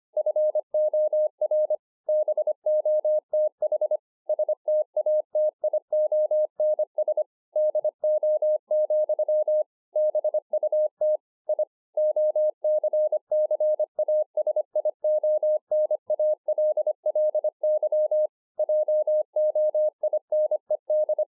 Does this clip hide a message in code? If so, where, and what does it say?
Yes: Morse code from 0.2 to 21.2 s, heard as a keyed tone, about 25 wpm, 610 Hz. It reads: FOR BOTH STATIONS DO, BUT I OCCASIONALLY JOINED